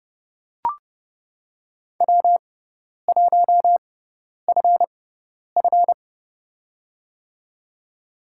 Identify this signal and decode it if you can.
Morse code, heard as a keyed tone, about 32 wpm, spaced as for 10 wpm, 700 Hz. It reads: W1FF
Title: USA Call Signs - 6x Character Spacing - 30wpm